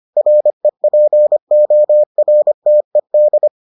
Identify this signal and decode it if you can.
Morse code, heard as a keyed tone, about 25 wpm, 600 Hz. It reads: REPORTED